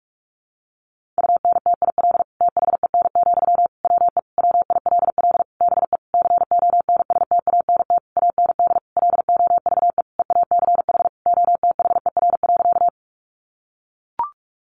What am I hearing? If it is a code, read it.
Morse code, heard as a keyed tone, about 44 wpm, 700 Hz. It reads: UNTIL THEN, WE WILL BE CONSTANT AND LOVE EACH OTHER.
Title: Sentences from Top 1500 Words 45wpm